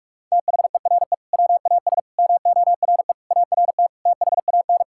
Morse code, heard as a keyed tone, about 42 wpm, 690 Hz. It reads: THERE WAS MORE ART THAN
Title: Sentences from Top 1000 Words - Review Part 6 - 45wpm